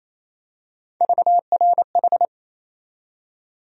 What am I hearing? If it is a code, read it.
Morse code, heard as a keyed tone, about 28 wpm, 700 Hz. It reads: VRH
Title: USA Call Sign Postfixes - 3 Letters 28wpm